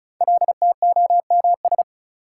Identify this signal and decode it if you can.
Morse code, heard as a keyed tone, about 36 wpm, 700 Hz. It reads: LTOMS
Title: Code Groups - 5 Letter:Numbers - 35wpm